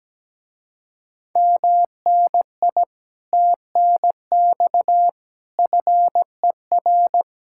Morse code, heard as a keyed tone, about 17 wpm, 700 Hz. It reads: MNI TNX FER
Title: Speed-Racing Common QSO Phrases - 17wpm